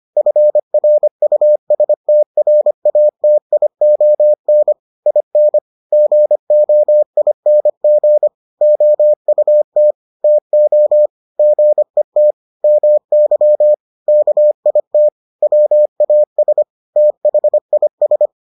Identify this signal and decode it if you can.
Morse code, heard as a keyed tone, about 25 wpm, 600 Hz. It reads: FRUSTRATION IN GOING OUT TO GET MY KIT WAS THIS